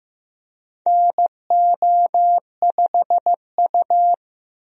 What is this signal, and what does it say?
Morse code, heard as a keyed tone, about 15 wpm, 700 Hz. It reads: NO5U
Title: USA Call Signs - 4 Characters - 15wpm